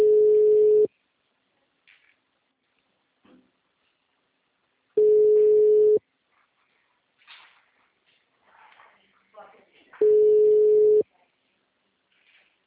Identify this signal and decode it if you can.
Ringback tone (Europe, 425 Hz), with speech.